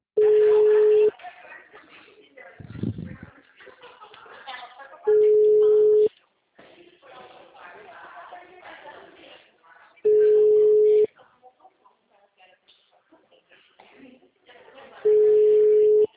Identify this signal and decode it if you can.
Ringback tone (Europe, 425 Hz), with speech.